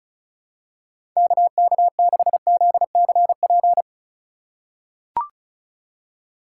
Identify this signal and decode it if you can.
Morse code, heard as a keyed tone, about 34 wpm, 700 Hz. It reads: KK6ZCP
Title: USA Call Signs - 3x Character Spacing - 35wpm